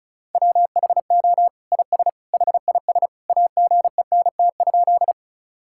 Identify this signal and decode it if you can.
Morse code, heard as a keyed tone, about 36 wpm, 700 Hz. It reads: WHO IS HIS AGENT?
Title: Sentences from Top 1600 Words 35wpm